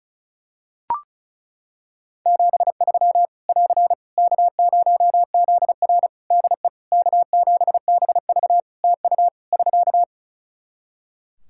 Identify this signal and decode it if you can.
Morse code, heard as a keyed tone, about 36 wpm, 700 Hz. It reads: 73 <AR> K0ZR DE K7BV TU <SK>
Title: K1USN SST QSO Practice 35wpm